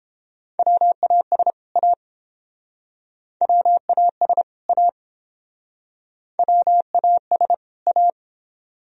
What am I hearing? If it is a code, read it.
Morse code, heard as a keyed tone, about 30 wpm, 700 Hz. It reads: WAS A WAS A WAS A